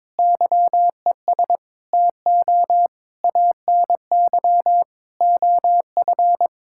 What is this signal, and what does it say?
Morse code, heard as a keyed tone, about 22 wpm, 700 Hz. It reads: YES TO ANY OF
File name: Sets of 9 Words - 22wpm